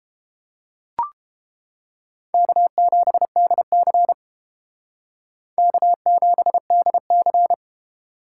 Morse code, heard as a keyed tone, about 30 wpm, 700 Hz. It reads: K7DC K7DC